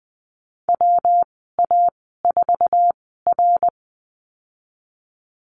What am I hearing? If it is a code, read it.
Morse code, heard as a keyed tone, about 20 wpm, spaced as for 14 wpm, 700 Hz. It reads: WA4R
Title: 154 Lesson 25g - Letter p - Call Signs 20wpm